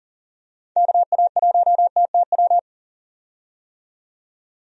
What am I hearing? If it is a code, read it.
Morse code, heard as a keyed tone, about 40 wpm, 700 Hz. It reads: KA1TTW
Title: USA Call Signs - 6 Characters - 40wpm